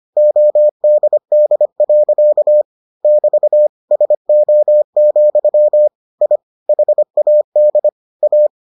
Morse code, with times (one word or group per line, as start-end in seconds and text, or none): 0.2-2.6 ODD.
3.0-3.7 <BT>
3.9-5.9 SO,
6.2-6.4 I
6.7-7.9 HAD
8.2-8.5 A